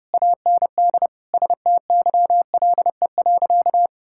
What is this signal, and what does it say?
Morse code, heard as a keyed tone, about 30 wpm, 700 Hz. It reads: AND STYLE.
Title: Sentences from Top 1000 Words - Review Part 5 - 30wpm